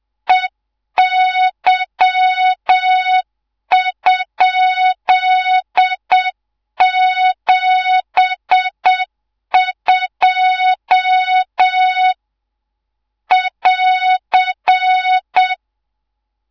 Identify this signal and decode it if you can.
Morse code, heard as a keyed tone, about 7 wpm, 3700 Hz, with noise. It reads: EY?72 <AR>
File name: Morze-HA2MN / hangp / 2 03 GYAKORLAT